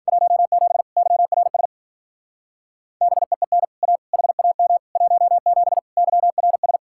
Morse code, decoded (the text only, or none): 17 YRS BEEN A HAM 17 YRS